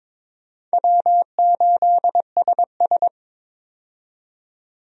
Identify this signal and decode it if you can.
Morse code, heard as a keyed tone, about 22 wpm, 700 Hz. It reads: W8SS
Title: USA Call Signs - 4 Characters - 22wpm